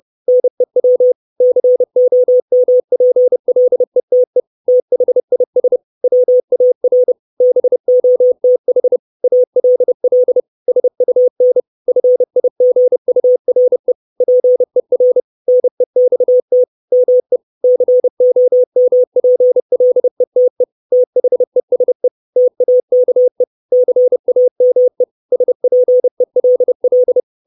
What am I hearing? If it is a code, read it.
Morse code, heard as a keyed tone, about 30 wpm, 500 Hz. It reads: NEW COMPLETE THIS WAR BOTH ALL SUN FIGURE PER NEXT ME COMPLETE THESE TAKE CAME SPELL
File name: 632 Words 30WPM